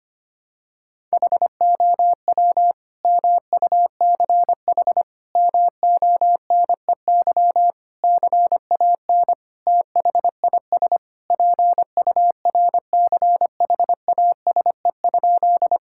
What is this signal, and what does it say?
Morse code, heard as a keyed tone, about 25 wpm, 700 Hz. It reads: HOW MUCH MONEY CAN THIS PURCHASE?
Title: Sentences from Top 1000 Words - Review Part 5 - 25wpm